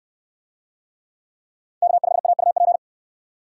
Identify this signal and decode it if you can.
Morse code, heard as a keyed tone, about 64 wpm, 700 Hz. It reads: K5AUP